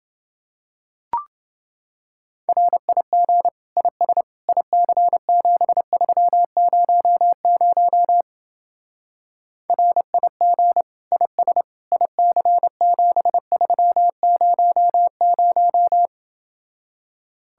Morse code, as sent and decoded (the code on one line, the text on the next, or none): .-. .. --. / .. ... / .. -.-. --... ...-- ----- ----- / .-. .. --. / .. ... / .. -.-. --... ...-- ----- -----
RIG IS IC7300 RIG IS IC7300